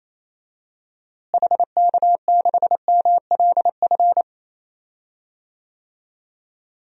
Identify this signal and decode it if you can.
Morse code, heard as a keyed tone, about 28 wpm, 700 Hz. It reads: HK6MLF
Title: DXCC-100 Call Signs - 6 Characters - Repeated 2x - 28wpm